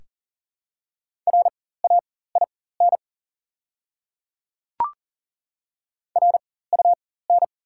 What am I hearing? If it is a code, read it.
Morse code, heard as a keyed tone, about 42 wpm, spaced as for 19 wpm, 690 Hz. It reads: RAIN RUN
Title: 4x Character Spacing - Top 300 Words - 40wpm